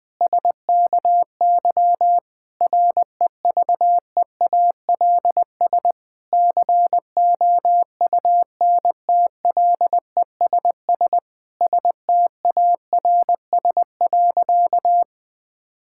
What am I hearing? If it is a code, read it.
Morse code, heard as a keyed tone, about 20 wpm, 700 Hz. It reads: SKY REVEALS COUNTLESS STARS.
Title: Difficult Sentences with Hints - 20wpm